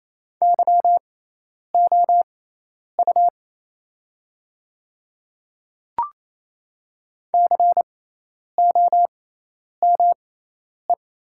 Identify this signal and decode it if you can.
Morse code, heard as a keyed tone, about 28 wpm, spaced as for 10 wpm, 700 Hz. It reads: YOU COME
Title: 6x Character Spacing - Top 100 Words - 28wpm